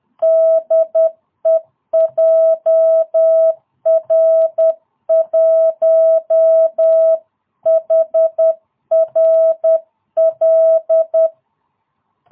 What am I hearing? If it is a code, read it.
Morse code, heard as a keyed tone, about 10 wpm, 660 Hz. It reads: DE JR1HRL